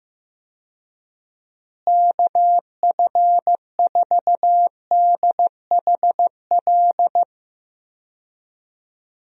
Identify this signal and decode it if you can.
Morse code, heard as a keyed tone, about 15 wpm, 700 Hz. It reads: KF4DHL
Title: USA Call Signs - 6 Characters - Repeated 2x - 15wpm